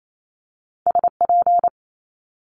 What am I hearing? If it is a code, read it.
Morse code, heard as a keyed tone, about 28 wpm, 700 Hz. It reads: SP
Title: Multi-letter Phonemes Quick Fire 28wpm